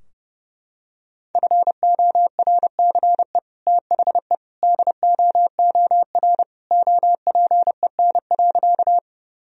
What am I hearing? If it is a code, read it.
Morse code, heard as a keyed tone, about 30 wpm, 700 Hz. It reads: FORCE THE DOOR OPEN.
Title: Sentences from Top 1000 Words - Review Part 3 - 30wpm